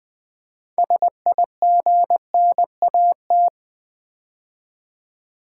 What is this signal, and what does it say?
Morse code, heard as a keyed tone, about 20 wpm, 700 Hz. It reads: SIGNAT
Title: Build A Word - 3 Letters At A Time - 9-Letter Words - 20wpm